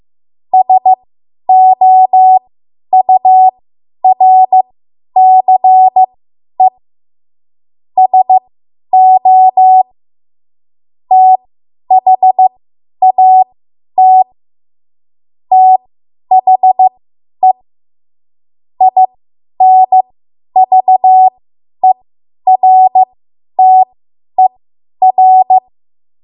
Morse code, heard as a keyed tone, about 15 wpm, spaced as for 10 wpm, 750 Hz. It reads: SOURCE SO THAT THE INVERTER
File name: Archive / 10 WPM / 10WPM